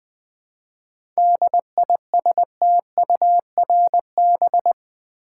Morse code, heard as a keyed tone, about 20 wpm, 700 Hz. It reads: DISTURB
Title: 7 Letter Words - First 3 Letters Given - 20wpm